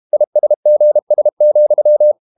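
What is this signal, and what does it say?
Morse code, heard as a keyed tone, about 32 wpm, 600 Hz. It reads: ISGS,
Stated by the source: Morse code practice at 32 wpm from CW med Gnister, reading Ken Follett's novel Never in English.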